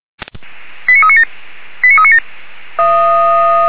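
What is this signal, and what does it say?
Five-tone selective call (ZVEI), heard as quick runs of five tones, tones of about 70 ms. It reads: 89397 89397
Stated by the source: ZVEI five-tone selective call received on an SDR.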